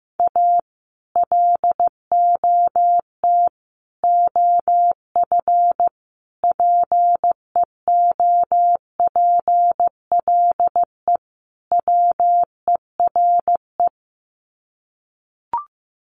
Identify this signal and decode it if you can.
Morse code, heard as a keyed tone, about 15 wpm, 700 Hz. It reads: A LOT OF PEOPLE WERE
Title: Sets of 5 Words Encore2 15wpm